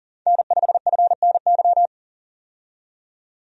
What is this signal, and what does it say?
Morse code, heard as a keyed tone, about 40 wpm, 700 Hz. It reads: N5FNY
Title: USA Call Signs - 5 Characters - 40wpm